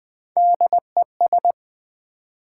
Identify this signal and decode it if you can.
Morse code, heard as a keyed tone, about 20 wpm, 700 Hz. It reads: DES
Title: Build A Word - 1 Letter At A Time - 4-Letter Words - 20wpm